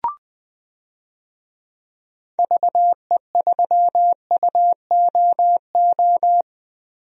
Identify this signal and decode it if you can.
Morse code, heard as a keyed tone, about 20 wpm, 700 Hz. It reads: VE3UOO
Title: DXCC-100 Call Signs - 6 Characters - Repeated 3x - 20wpm